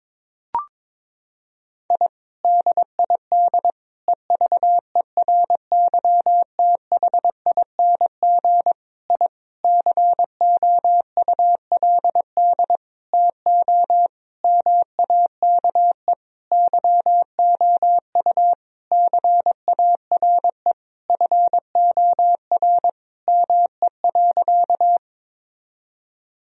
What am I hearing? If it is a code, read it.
Morse code, heard as a keyed tone, about 22 wpm, 700 Hz. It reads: I DID EVERYTHING I COULD TO MAKE YOU CARE FOR ME.
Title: Sentences from Top 1000 Words - Review Part 1 - 22wpm